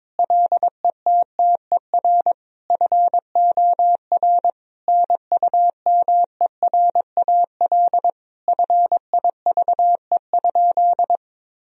Morse code, heard as a keyed tone, about 22 wpm, 700 Hz. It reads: LETTER FOR NUMERAL FIVE?
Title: Sentences from Top 1000 Words - Review Part 2 - 22wpm